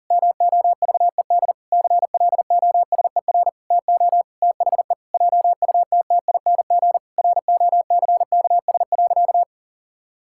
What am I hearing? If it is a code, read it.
Morse code, heard as a keyed tone, about 40 wpm, 700 Hz. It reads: MOVED CLOSER TO THE JUTTING ROCKS.
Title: Pangram Challenge v2 - 40wpm